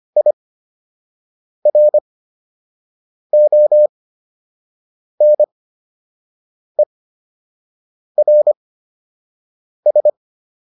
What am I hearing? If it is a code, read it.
Morse code, heard as a keyed tone, about 25 wpm, spaced as for 6 wpm, 600 Hz. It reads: IRONERS